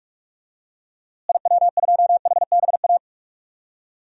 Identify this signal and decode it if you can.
Morse code, heard as a keyed tone, about 42 wpm, 700 Hz. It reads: IW2HBA
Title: DXCC-100 Call Signs - 6 Characters - Repeated 3x - 45wpm